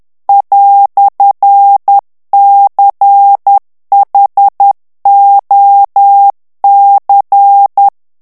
Morse code, heard as a keyed tone, about 10 wpm, 800 Hz. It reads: "CHOC
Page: Français - texte en clair 1 - vitesse : 10 mots minute